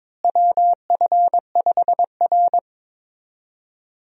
Morse code, heard as a keyed tone, about 22 wpm, 700 Hz. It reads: WF5R
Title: USA Call Signs - 4 Characters - Repeated 3x - 22wpm